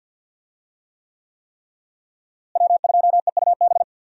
Morse code, heard as a keyed tone, about 48 wpm, 690 Hz. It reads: W2EUB
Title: USA Call Signs - 5 Characters - Repeated 3x - 50wpm